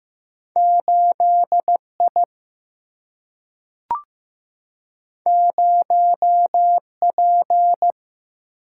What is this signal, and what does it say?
Morse code, heard as a keyed tone, about 15 wpm, 700 Hz. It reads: 8 I 0 P
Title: Single-Letter Single-Number 15wpm